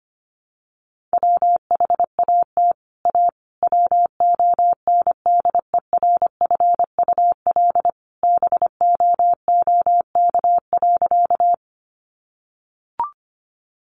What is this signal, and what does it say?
Morse code, heard as a keyed tone, about 25 wpm, 700 Hz. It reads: WHAT A WONDERFUL BOOK.
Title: Sentences from Top 1100 Words 25wpm